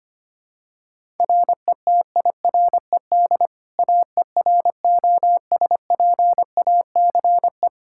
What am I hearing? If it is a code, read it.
Morse code, heard as a keyed tone, about 25 wpm, 700 Hz. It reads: RETIRED AEROSPACE
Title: K1USN SST QSO Practice 25wpm